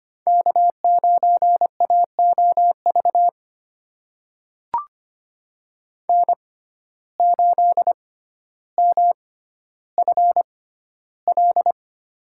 Morse code, heard as a keyed tone, about 25 wpm, spaced as for 13 wpm, 700 Hz. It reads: K9AOV N8MFL
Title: USA Call Signs - 5 Characters - 6x Character Spacing - 25wpm